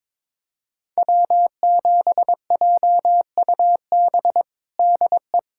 Morse code, heard as a keyed tone, about 23 wpm, 700 Hz. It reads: W7JUB DE
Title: USA Callsign DE Callsign 22wpm